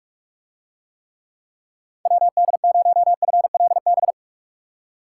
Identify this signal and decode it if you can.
Morse code, heard as a keyed tone, about 43 wpm, 700 Hz. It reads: WD0FLB